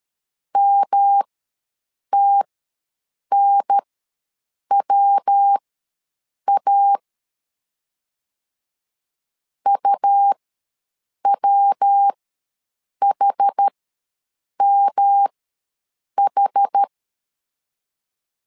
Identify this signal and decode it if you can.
Morse code, heard as a keyed tone, about 13 wpm, spaced as for 7 wpm, 800 Hz. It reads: MTNWA UWHMH
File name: LEZIONI / 3 MWRO / 3 MW ws 12wpm 1816